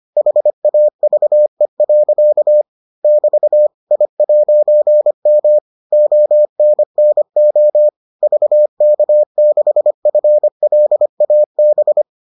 Morse code, heard as a keyed tone, about 25 wpm, 600 Hz. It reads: HAVE. <BT> I'M ONNO VK6FLAB